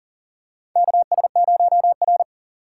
Morse code, heard as a keyed tone, about 40 wpm, 700 Hz. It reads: KS0R
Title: USA Call Signs - 4 Characters - 3x Character Spacing - 40wpm